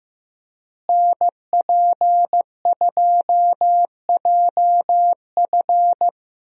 Morse code, heard as a keyed tone, about 15 wpm, 700 Hz. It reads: NP2JF
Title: USA Call Signs - 5 Characters - Repeated 2x - 15wpm